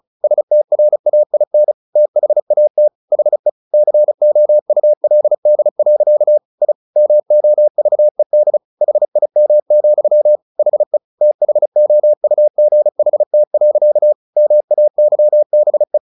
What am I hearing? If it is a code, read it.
Morse code, heard as a keyed tone, about 35 wpm, 600 Hz. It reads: STRAIN THAT HE COULD. I MOVED HIM, HE THOUGHT. MAYBE
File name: the old man and the sea 2 35wpm